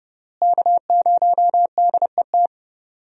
Morse code, heard as a keyed tone, about 31 wpm, 700 Hz. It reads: K0DET
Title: USA Call Signs - 5 Characters - Repeated 3x - 30wpm